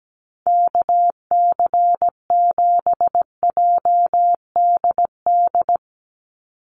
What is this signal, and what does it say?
Morse code, heard as a keyed tone, about 17 wpm, 700 Hz. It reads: KC7JDD
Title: USA Call Signs 17wpm